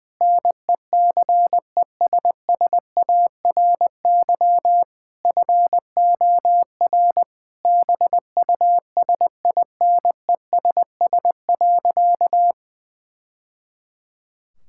Morse code, heard as a keyed tone, about 20 wpm, 700 Hz. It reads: NECESSARY FOR BUSINESS.
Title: Sentences from Top 1000 Words - Review Part 5 - 20wpm